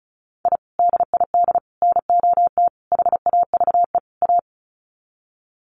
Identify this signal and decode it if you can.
Morse code, heard as a keyed tone, about 34 wpm, 700 Hz. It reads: I DID NOT HAVE A
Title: Sets of 5 Words 35wpm